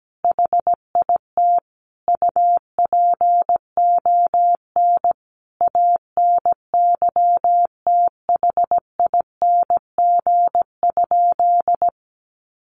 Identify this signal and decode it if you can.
Morse code, heard as a keyed tone, about 17 wpm, 700 Hz. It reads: HIT UPON ANYTHING?
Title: Sentences from Top 1200 Words 17wpm